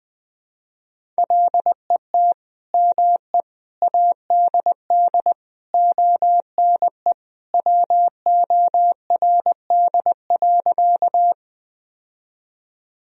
Morse code, heard as a keyed tone, about 20 wpm, 700 Hz. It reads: LET ME ADD ONE WORD.
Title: Sentences from Top 1000 Words - Review Part 1 - 20wpm